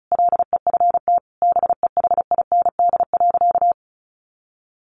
Morse code, heard as a keyed tone, about 34 wpm, 700 Hz. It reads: LEFT BEHIND.
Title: A Sound Of Thunder 35wpm